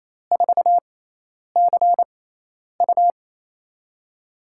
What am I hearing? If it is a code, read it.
Morse code, heard as a keyed tone, about 28 wpm, spaced as for 10 wpm, 700 Hz. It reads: 4CU
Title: USA Call Signs - 4 Characters - 6x Character Spacing - 28wpm